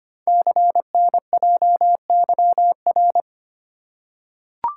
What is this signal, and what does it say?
Morse code, heard as a keyed tone, about 25 wpm, 700 Hz. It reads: CNJYR
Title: Code Groups - 5 Letters - 25wpm